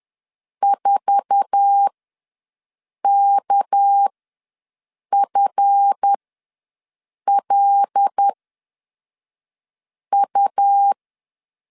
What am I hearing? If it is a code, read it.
Morse code, heard as a keyed tone, about 10 wpm, spaced as for 5 wpm, 800 Hz. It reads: 4KFLU
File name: LEZIONI / 8 YJ45 / 8 YJ4 ws 10wpm 4020